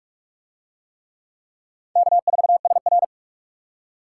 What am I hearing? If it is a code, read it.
Morse code, heard as a keyed tone, about 41 wpm, 700 Hz. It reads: K4SR